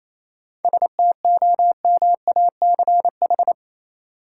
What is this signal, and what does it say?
Morse code, heard as a keyed tone, about 28 wpm, 700 Hz. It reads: STOMACH
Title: Seven Letter Words - 28wpm